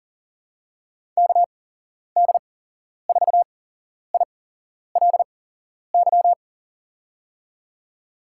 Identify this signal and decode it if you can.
Morse code, heard as a keyed tone, about 40 wpm, spaced as for 11 wpm, 700 Hz. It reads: KD4ILY
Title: USA Call Signs - 6 Characters - 8x Character Spacing - 40wpm